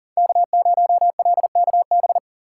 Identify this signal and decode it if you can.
Morse code, heard as a keyed tone, about 39 wpm, 700 Hz. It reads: K0LKB
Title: USA Call Signs - 2x Character Spacing - 40wpm